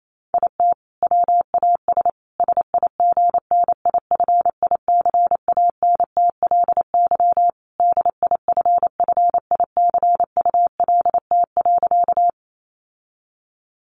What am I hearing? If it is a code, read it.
Morse code, heard as a keyed tone, about 28 wpm, 700 Hz. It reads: IT WAS SIGNIFICANTLY DIFFICULT.
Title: Sentences from Top 1500 Words 28wpm